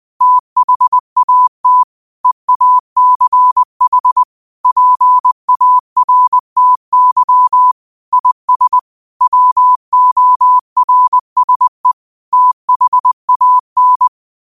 Morse code, heard as a keyed tone, about 20 wpm, 1000 Hz. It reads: THAT EACH PARTY IS WORSE THAN
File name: qotdfiles / 20wpm